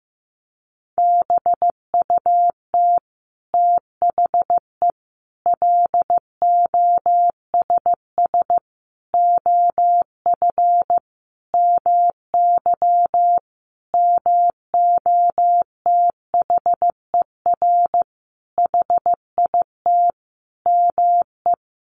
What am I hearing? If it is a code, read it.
Morse code, heard as a keyed tone, about 15 wpm, 700 Hz. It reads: BUT THE LOSS OF MY MOTHER HIT ME
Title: Sentences from Top 1400 Words 15wpm